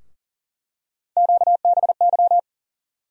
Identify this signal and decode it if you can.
Morse code, heard as a keyed tone, about 38 wpm, 700 Hz. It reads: QBY
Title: USA Call Sign Postfixes - 3 Letters 40wpm